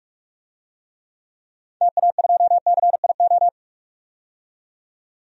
Morse code, heard as a keyed tone, about 44 wpm, 700 Hz. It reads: TA2CIO
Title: DXCC-100 Call Signs - 6 Characters - 45wpm